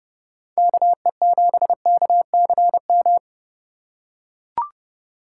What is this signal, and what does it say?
Morse code, heard as a keyed tone, about 30 wpm, 700 Hz. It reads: KE7KCM
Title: USA Call Signs - 2x Character Spacing - 30wpm